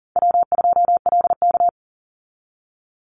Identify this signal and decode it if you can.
Morse code, heard as a keyed tone, about 40 wpm, 700 Hz. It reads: W2LK
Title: Easy Call Signs Repeated 2x - 40wpm